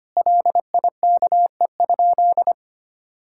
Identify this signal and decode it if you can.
Morse code, heard as a keyed tone, about 25 wpm, 700 Hz. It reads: LIKE?
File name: practice / 25wpm / Sentences with 3-Word Phrases - 25wpm